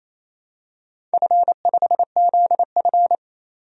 Morse code, heard as a keyed tone, about 28 wpm, 700 Hz. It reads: F5ZF
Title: DXCC-100 Call Signs - 4 Characters - 28wpm